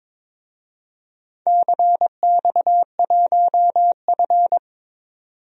Morse code, heard as a keyed tone, about 22 wpm, 700 Hz. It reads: CX1F
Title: DXCC-100 Call Signs - 4 Characters - 22wpm